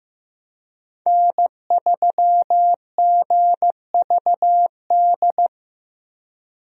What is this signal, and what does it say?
Morse code, heard as a keyed tone, about 14 wpm, 700 Hz. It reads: N3GVD
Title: USA Call Signs - 5 Characters - Repeated 2x - 15wpm